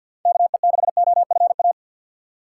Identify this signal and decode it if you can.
Morse code, heard as a keyed tone, about 52 wpm, 700 Hz. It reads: KE6OFA
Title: USA Call Signs - 6 Characters - 6x Character Spacing - 50wpm